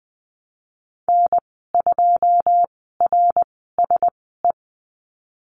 Morse code, heard as a keyed tone, about 20 wpm, spaced as for 15 wpm, 700 Hz. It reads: N2RSE
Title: 128 Lesson 21g - Number 6 - Call Signs 20wpm